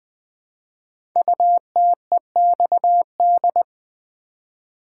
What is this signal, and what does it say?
Morse code, heard as a keyed tone, about 21 wpm, 700 Hz. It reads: UTEXD
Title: Code Groups - 5 Letters - 20wpm